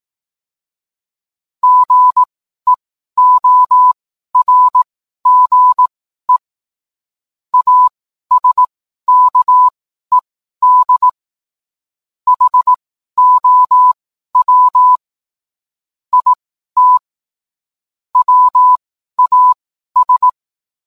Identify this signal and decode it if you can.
Morse code, heard as a keyed tone, about 18 wpm, spaced as for 12 wpm, 1000 Hz. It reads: GEORGE ASKED HOW IT WAS